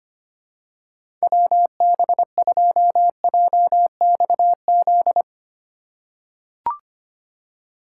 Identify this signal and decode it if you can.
Morse code, heard as a keyed tone, about 25 wpm, 700 Hz. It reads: WB2JXZ
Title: USA Call Signs - 6 Characters - 4x Character Spacing - 25wpm